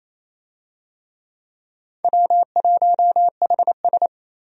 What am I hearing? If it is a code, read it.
Morse code, heard as a keyed tone, about 27 wpm, 700 Hz. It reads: W1HS
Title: USA Call Signs - 4 Characters - 28wpm